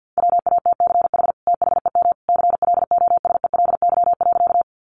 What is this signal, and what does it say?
Morse code, heard as a keyed tone, about 50 wpm, 700 Hz. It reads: WATCH THEM CLOSELY.
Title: Sentences from Top 1400 Words 50wpm